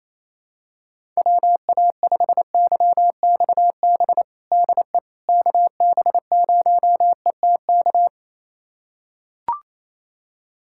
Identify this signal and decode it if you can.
Morse code, heard as a keyed tone, about 28 wpm, 700 Hz. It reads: WA5YXB DE KB0ETK
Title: USA Callsign DE Callsign - 6 Characters - 28wpm